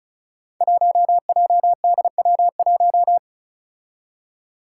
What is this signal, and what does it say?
Morse code, heard as a keyed tone, about 34 wpm, 700 Hz. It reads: 1JDW1
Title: Five Challenging Letter-Number Combinations 35wpm